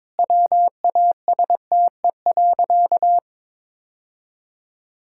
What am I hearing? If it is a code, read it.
Morse code, heard as a keyed tone, about 22 wpm, 700 Hz. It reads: WASTE.
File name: practice / 22wpm / Sentences with 3-Word Phrases - 22wpm